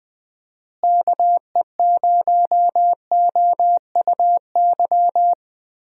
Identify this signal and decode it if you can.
Morse code, heard as a keyed tone, about 20 wpm, 700 Hz. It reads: KE0OUY
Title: USA Call Signs - 6 Characters - Repeated 3x - 20wpm